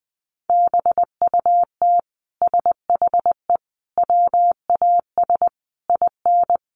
Morse code, heard as a keyed tone, about 19 wpm, 700 Hz. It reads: BUT SHE WAS IN
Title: Sentences from Top 1200 Words 20wpm